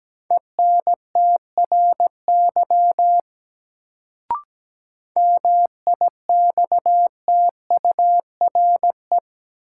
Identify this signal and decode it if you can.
Morse code, heard as a keyed tone, about 17 wpm, 700 Hz. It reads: ENTRY MIXTURE
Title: Sentences from Top 1500 Words - New Word Drill Alt - 17wpm